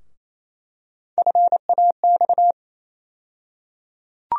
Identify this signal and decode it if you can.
Morse code, heard as a keyed tone, about 28 wpm, 700 Hz. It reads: FAX
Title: USA Call Sign Postfixes - 3 Letters 28wpm